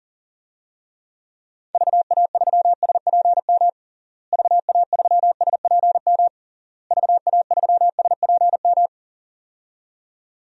Morse code, heard as a keyed tone, about 40 wpm, 700 Hz. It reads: VA3SPM VA3SPM VA3SPM
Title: DXCC-100 Call Signs - 6 Characters - Repeated 3x - 40wpm